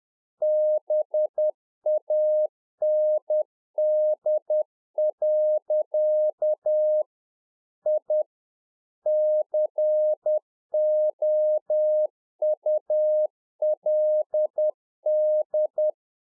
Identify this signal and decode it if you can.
Morse code, heard as a keyed tone, about 10 wpm, 610 Hz. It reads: BAND. I COULD
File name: 2011-04-mp3 / RICK JJ0MPI 10WPM